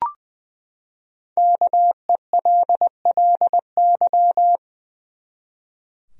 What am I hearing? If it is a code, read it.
Morse code, heard as a keyed tone, about 20 wpm, 700 Hz. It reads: KELLY
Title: USA Common Names - Max 5 Letters - 20wpm